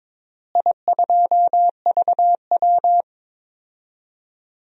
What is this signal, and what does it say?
Morse code, heard as a keyed tone, about 22 wpm, 700 Hz. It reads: I2VW
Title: DXCC-100 Call Signs - 4 Characters - Repeated 3x - 22wpm